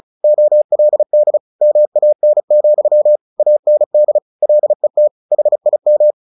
Morse code, heard as a keyed tone, about 34 wpm, 600 Hz. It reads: OLD MAN, AND LET HIM
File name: the old man and the sea 2 35wpm